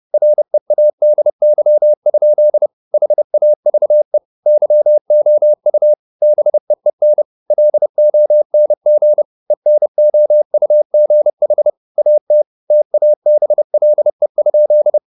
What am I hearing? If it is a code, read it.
Morse code, heard as a keyed tone, about 30 wpm, 600 Hz. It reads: READY? HAVE YOU BEEN LONG ENOUGH AT TABLE?
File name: the old man and the sea / the old man and the sea 1 30wpm